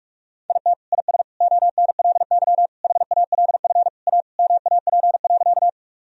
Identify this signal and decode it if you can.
Morse code, heard as a keyed tone, about 46 wpm, 700 Hz. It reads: IT IS ONLY HALF A MAP.
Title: Sentences from Top 1000 Words - Review Part 2 - 45wpm